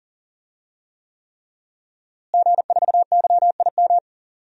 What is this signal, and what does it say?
Morse code, heard as a keyed tone, about 40 wpm, 700 Hz. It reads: G4YIM
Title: DXCC-100 Call Signs - 5 Characters - Repeated 3x - 40wpm